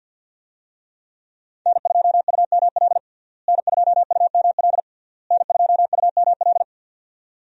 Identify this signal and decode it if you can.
Morse code, heard as a keyed tone, about 46 wpm, 700 Hz. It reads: N2UML N2UML N2UML
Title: USA Call Signs - 5 Characters - Repeated 3x - 50wpm